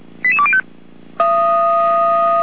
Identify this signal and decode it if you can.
Five-tone selective call (ZVEI), heard as quick runs of five tones, tones of about 70 ms. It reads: 88205